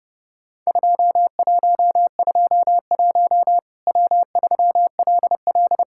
Morse code, heard as a keyed tone, about 30 wpm, 700 Hz. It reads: 2121 W3LL
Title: Worked All Europe DX Contest - QTC Exchange - 30wpm